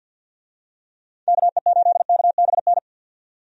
Morse code, heard as a keyed tone, about 50 wpm, 700 Hz. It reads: KE8KBN
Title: USA Call Signs - 6 Characters - 50wpm